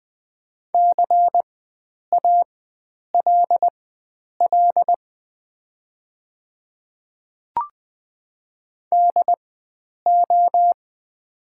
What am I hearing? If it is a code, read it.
Morse code, heard as a keyed tone, about 20 wpm, spaced as for 9 wpm, 690 Hz. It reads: CALL DO
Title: 4x Character Spacing - Top 100 Words - 20wpm